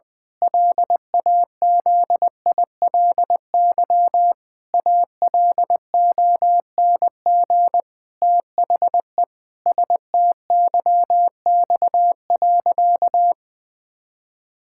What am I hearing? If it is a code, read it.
Morse code, heard as a keyed tone, about 21 wpm, 700 Hz. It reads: LAZILY ALONG THE STYX.
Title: Pangram Challenge v2 - 20wpm